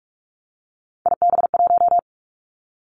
Morse code, heard as a keyed tone, about 46 wpm, 700 Hz. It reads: IB1
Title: Three Challenging Letter-Number Combinations 45wpm